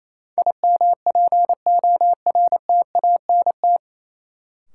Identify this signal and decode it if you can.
Morse code, heard as a keyed tone, about 28 wpm, 700 Hz. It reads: IMPORTANT
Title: Sets of 3 Words - New Word Drill - 28wpm